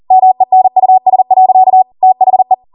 Morse code, heard as a keyed tone, about 40 wpm, 750 Hz. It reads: MENUS. THE
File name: Morse / 40WPM